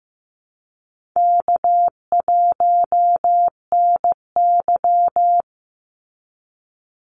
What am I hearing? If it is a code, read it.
Morse code, heard as a keyed tone, about 15 wpm, 700 Hz. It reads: K1NY
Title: Easy Call Signs Repeated 3x - 15wpm